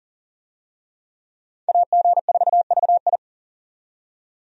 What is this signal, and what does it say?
Morse code, heard as a keyed tone, about 40 wpm, 700 Hz. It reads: AG4VI
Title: USA Call Signs - 5 Characters - 40wpm